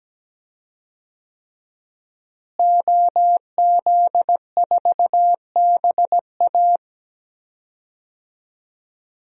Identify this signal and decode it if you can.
Morse code, heard as a keyed tone, about 17 wpm, 700 Hz. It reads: OZ4BA
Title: DXCC-100 Call Signs - 5 Characters - Repeated 3x - 17wpm